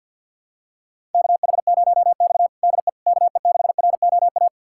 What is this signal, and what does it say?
Morse code, heard as a keyed tone, about 51 wpm, 700 Hz. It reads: KH0X DE KE6ROA
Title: USA Callsign DE Callsign 50wpm